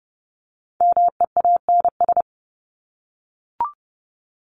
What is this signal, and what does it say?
Morse code, heard as a keyed tone, about 30 wpm, 700 Hz. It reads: MEANS
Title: ES:S Suffix - Top 100 Words with Base Words 30wpm